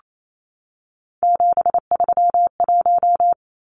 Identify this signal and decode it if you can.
Morse code, heard as a keyed tone, about 28 wpm, 700 Hz. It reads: 731
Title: Numbers - 3-Digit 28wpm